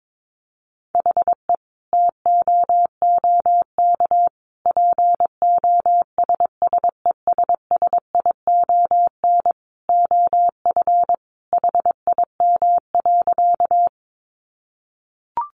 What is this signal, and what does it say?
Morse code, heard as a keyed tone, about 22 wpm, 700 Hz. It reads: HE TOOK POSSESSION OF HIM.
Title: Sentences from Top 1400 Words 22wpm